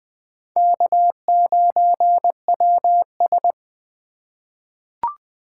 Morse code, heard as a keyed tone, about 20 wpm, 700 Hz. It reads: K9WS
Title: USA Call Signs - 2x Character Spacing - 20wpm